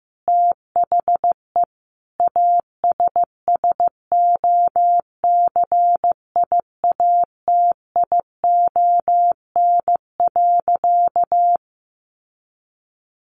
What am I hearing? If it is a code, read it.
Morse code, heard as a keyed tone, about 15 wpm, 700 Hz. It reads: THE ASSOCIATION.
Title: Sentences from Top 1200 Words 15wpm